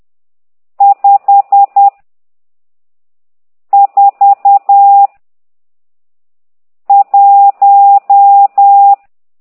Morse code, heard as a keyed tone, about 10 wpm, 800 Hz. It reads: 5 4 1
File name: ct2idw / exerc12